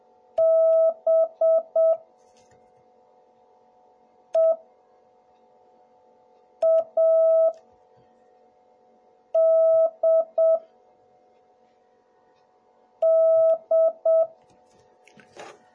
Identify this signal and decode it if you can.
Morse code, heard as a keyed tone, about 7 wpm, spaced as for 3 wpm, 660 Hz. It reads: BEADD